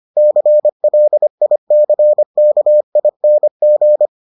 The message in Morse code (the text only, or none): CLICKING